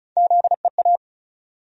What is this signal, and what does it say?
Morse code, heard as a keyed tone, about 34 wpm, 700 Hz. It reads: ZEA
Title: USA Call Sign Postfixes - 3 Letters 35wpm